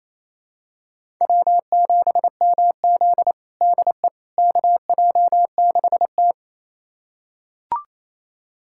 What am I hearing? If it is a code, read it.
Morse code, heard as a keyed tone, about 28 wpm, 700 Hz. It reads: W7MZ DE KJ6T
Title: USA Callsign DE Callsign - 4 Characters - 28wpm